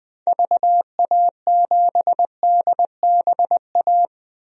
Morse code, heard as a keyed tone, about 20 wpm, 700 Hz. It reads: VA7DBA
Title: DXCC-100 Call Signs - 6 Characters - 20wpm